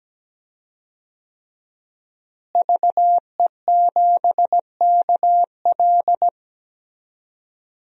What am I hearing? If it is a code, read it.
Morse code, heard as a keyed tone, about 17 wpm, 700 Hz. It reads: VE7KL
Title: DXCC-100 Call Signs - 5 Characters - Repeated 3x - 17wpm